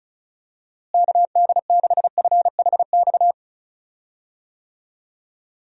Morse code, heard as a keyed tone, about 34 wpm, 700 Hz. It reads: KD6FHX
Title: USA Call Signs - 6 Characters - Repeated 2x - 35wpm